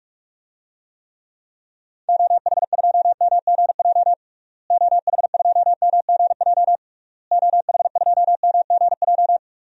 Morse code, heard as a keyed tone, about 41 wpm, 700 Hz. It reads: OH2MGJ OH2MGJ OH2MGJ
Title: DXCC-100 Call Signs - 6 Characters - Repeated 3x - 45wpm